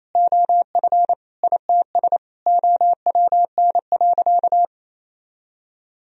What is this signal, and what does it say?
Morse code, heard as a keyed tone, about 28 wpm, 700 Hz. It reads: OF ITS OWN.
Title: Sentences from Top 1000 Words - Review Part 4 - 28wpm